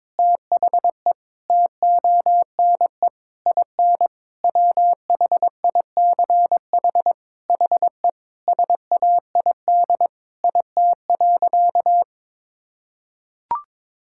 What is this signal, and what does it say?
Morse code, heard as a keyed tone, about 22 wpm, 700 Hz. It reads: THE TONE IN WHICH HE SAID IT.
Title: Sentences from Top 1600 Words 22wpm